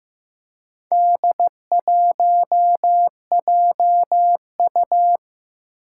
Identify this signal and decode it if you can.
Morse code, heard as a keyed tone, about 15 wpm, 700 Hz. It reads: D1JU
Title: Four Challenging Letter-Number Combinations 15wpm